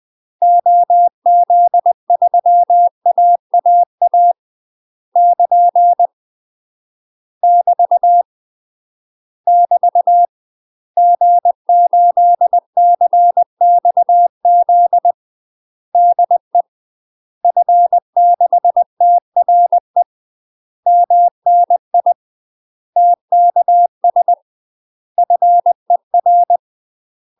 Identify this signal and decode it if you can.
Morse code, heard as a keyed tone, about 20 wpm, 700 Hz. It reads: OZ3AAA <KN> <BT> <BT> G8CXZ DE F6TRE MNI TKS FER
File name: UK MASTER QSOs TRAINING/20WPM